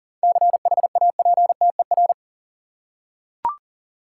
Morse code, heard as a keyed tone, about 40 wpm, 700 Hz. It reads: CHAPTER
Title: Seven Letter Words - 40wpm